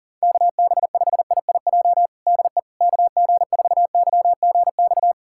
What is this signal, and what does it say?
Morse code, heard as a keyed tone, about 40 wpm, 700 Hz. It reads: KB5IIJ DE KG4YGX
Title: USA Callsign DE Callsign - 6 Characters - 40wpm